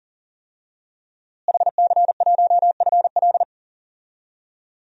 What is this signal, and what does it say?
Morse code, heard as a keyed tone, about 37 wpm, 700 Hz. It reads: HC1FL
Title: DXCC-100 Call Signs - 5 Characters - Repeated 2x - 40wpm